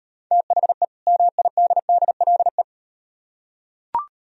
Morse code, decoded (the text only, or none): THE MIDDLE